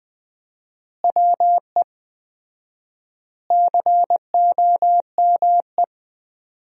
Morse code, heard as a keyed tone, about 20 wpm, 700 Hz. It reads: WE COME
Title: 4x Word-Spacing - Sentences from Top 100 Words - 20wpm